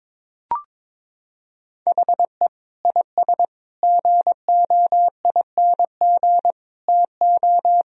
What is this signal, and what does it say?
Morse code, heard as a keyed tone, about 21 wpm, 700 Hz. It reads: HE IS GOING TO
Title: Sentences from Top 1000 Words - Review Part 4 - 22wpm